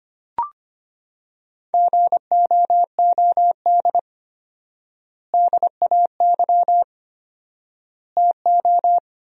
Morse code, heard as a keyed tone, about 25 wpm, 700 Hz. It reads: GOOD DAY TO
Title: 4x Word-Spacing - Sentences from Top 200 Words - 25wpm